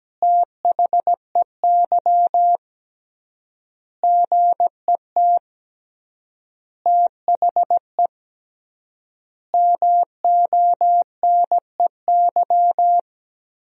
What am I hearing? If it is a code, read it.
Morse code, heard as a keyed tone, about 17 wpm, 700 Hz. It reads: THEY GET THE MONEY
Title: 3x Word-Spacing - Sentences from Top 400 Words - 17wpm